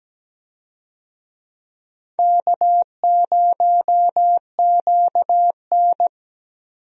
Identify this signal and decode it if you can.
Morse code, heard as a keyed tone, about 17 wpm, 700 Hz. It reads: K0QN
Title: USA Call Signs - 4 Characters - 17wpm